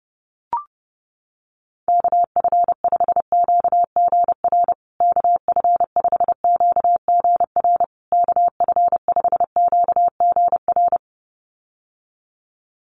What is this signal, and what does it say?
Morse code, heard as a keyed tone, about 30 wpm, 700 Hz. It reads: KF5QGR KF5QGR KF5QGR
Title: USA Call Signs Repeated 3x 30wpm